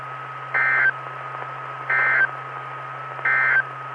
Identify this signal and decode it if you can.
Emergency Alert System (SAME) tones, heard as harsh buzzing bursts. It is heard from 0.4 to 3.7 s.
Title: Saline, AK SKYWARN Repeater System - W5RHS